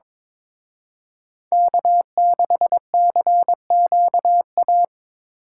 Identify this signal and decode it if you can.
Morse code, heard as a keyed tone, about 26 wpm, 700 Hz. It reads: K6CQA
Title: Speed Racing - USA 1x3 Call Signs 22wpm